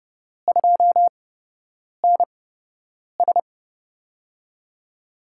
Morse code, heard as a keyed tone, about 30 wpm, spaced as for 8 wpm, 700 Hz. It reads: 2NS
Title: USA Call Signs - 8x Character Spacing - 30wpm